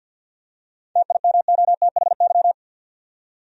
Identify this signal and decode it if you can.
Morse code, heard as a keyed tone, about 47 wpm, 700 Hz. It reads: TIMOTHY
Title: USA Common Names - 50wpm